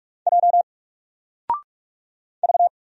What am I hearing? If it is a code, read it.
Morse code, heard as a keyed tone, about 46 wpm, 700 Hz. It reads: J V
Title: Single Letter-Number 45wpm